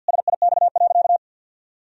Morse code, heard as a keyed tone, about 52 wpm, 700 Hz. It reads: SIX.
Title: Sentences from Top 1000 Words - Review Part 2 - 50wpm